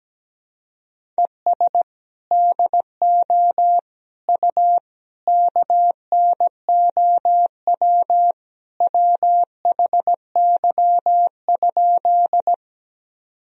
Morse code, heard as a keyed tone, about 17 wpm, 700 Hz. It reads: ES DO U KNOW WHY?
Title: Sentences from Top 200 Words and Abbreviations 17wpm